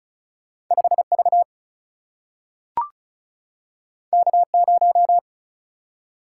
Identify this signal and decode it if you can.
Morse code, heard as a keyed tone, about 36 wpm, 700 Hz. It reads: 5 V K 0
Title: Single-Letter Single-Number 35wpm